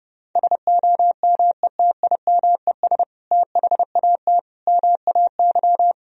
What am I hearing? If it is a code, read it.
Morse code, heard as a keyed tone, about 30 wpm, 700 Hz. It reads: SOMETIMES THAT MAY
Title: Sentences from Top 1000 Words - Review Part 3 - 30wpm